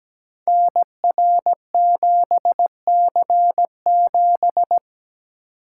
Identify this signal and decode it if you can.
Morse code, heard as a keyed tone, about 17 wpm, 700 Hz. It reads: NR7C7
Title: Code Groups - 5 Letter:Numbers - 17wpm